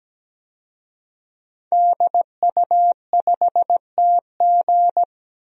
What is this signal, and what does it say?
Morse code, heard as a keyed tone, about 17 wpm, 700 Hz. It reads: DU5TG
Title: DXCC-100 Call Signs - 5 Characters - 17wpm